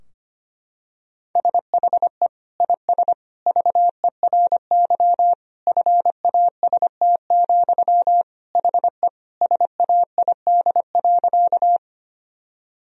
Morse code, heard as a keyed tone, about 25 wpm, 700 Hz. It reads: SHE IS VERY FAST, HE SAID.
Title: Sentences from Top 1000 Words - Review Part 2 - 25wpm